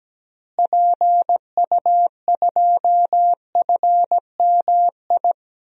Morse code, heard as a keyed tone, about 17 wpm, 700 Hz. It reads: PU2FMI
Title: DXCC-100 Call Signs - 6 Characters - Repeated 3x - 17wpm